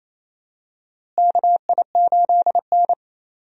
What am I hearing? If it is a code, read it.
Morse code, heard as a keyed tone, about 28 wpm, 700 Hz. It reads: KI8N
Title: USA Call Signs - 4 Characters - Repeated 2x - 28wpm